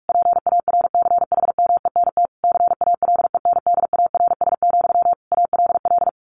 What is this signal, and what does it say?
Morse code, heard as a keyed tone, about 44 wpm, 700 Hz. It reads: PARCHMENT CALENDARS, ALL
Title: A Sound Of Thunder 45wpm